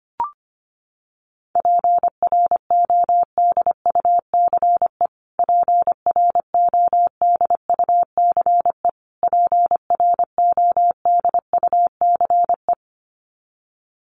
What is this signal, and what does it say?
Morse code, heard as a keyed tone, about 25 wpm, 700 Hz. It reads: PRODUCE PRODUCE PRODUCE
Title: Top 500 Words Repeated 25wpm